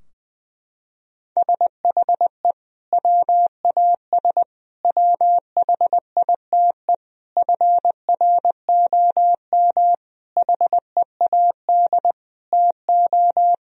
Morse code, heard as a keyed tone, about 20 wpm, 700 Hz. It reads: SHE WAS WHITE FROM HEAD TO
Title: Sentences from Top 1000 Words - Review Part 3 - 20wpm